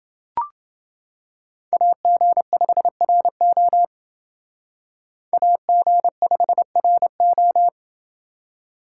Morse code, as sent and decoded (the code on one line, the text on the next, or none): .- --. ..... .-. --- / .- --. ..... .-. ---
AG5RO AG5RO